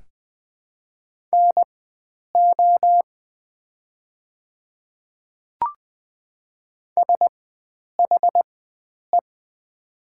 Morse code, heard as a keyed tone, about 20 wpm, spaced as for 9 wpm, 700 Hz. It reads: NO SHE
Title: 4x Character Spacing - Top 100 Words - 20wpm